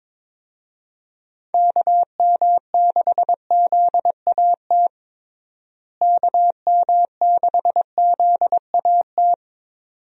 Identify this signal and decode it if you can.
Morse code, heard as a keyed tone, about 22 wpm, 700 Hz. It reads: KM6ZAT KM6ZAT
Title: USA Call Signs - 6 Characters - Repeated 2x - 22wpm